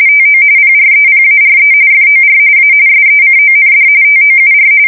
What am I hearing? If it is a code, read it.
RTTY, about 45.45 baud. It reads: TO PREPARE FOR A SECOND TOUR
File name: fsk170 / USS-Missouri